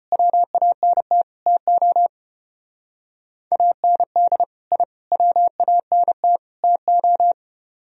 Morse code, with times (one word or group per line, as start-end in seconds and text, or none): 0.1-1.2 WANT
1.5-2.1 TO
3.5-4.4 AND
4.7-4.8 I
5.1-6.4 WANT
6.6-7.3 TO